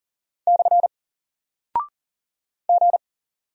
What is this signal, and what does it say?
Morse code, heard as a keyed tone, about 40 wpm, 700 Hz. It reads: / G
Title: Single Character (Letter:Number:Punctuation:Prosign) Alt 40wpm